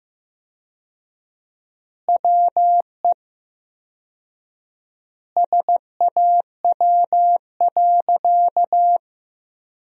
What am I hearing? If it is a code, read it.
Morse code, heard as a keyed tone, about 15 wpm, 700 Hz. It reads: WE SAW.
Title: 4x Word-Spacing - Sentences from Top 300 Words - 15wpm